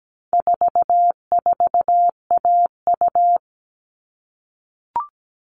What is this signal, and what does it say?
Morse code, heard as a keyed tone, about 17 wpm, 700 Hz. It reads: 44AU
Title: All Cut Numbers - 4-Digit 17wpm